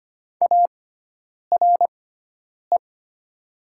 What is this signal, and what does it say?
Morse code, heard as a keyed tone, about 25 wpm, spaced as for 9 wpm, 700 Hz. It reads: ARE
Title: 6x Character Spacing - Top 300 Words - 25wpm